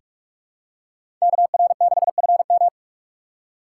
Morse code, heard as a keyed tone, about 43 wpm, 700 Hz. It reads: KR6FM
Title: USA Call Signs - 5 Characters - 6x Character Spacing - 45wpm